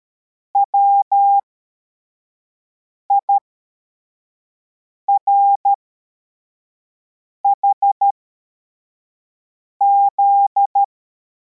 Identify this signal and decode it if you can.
Morse code, heard as a keyed tone, about 13 wpm, spaced as for 4 wpm, 800 Hz. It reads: WIRHZ
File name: formation / cw / Text74